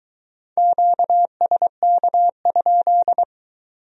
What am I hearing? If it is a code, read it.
Morse code, heard as a keyed tone, about 22 wpm, 700 Hz. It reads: QSK?